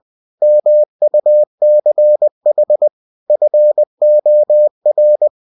Morse code, heard as a keyed tone, about 20 wpm, 600 Hz. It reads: MUCH FOR
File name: cw-books / winnie-the-pooh / winnie-20wpm-page-0048